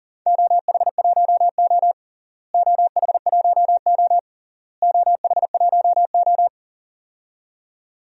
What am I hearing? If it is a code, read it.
Morse code, heard as a keyed tone, about 40 wpm, 700 Hz. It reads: OH1O OH1O OH1O
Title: DXCC-100 Call Signs - 4 Characters - Repeated 3x - 40wpm